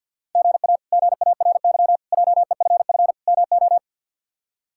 Morse code, heard as a keyed tone, about 52 wpm, 700 Hz. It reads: GA GARY JEFF MO
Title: K1USN SST QSO Practice - 50wpm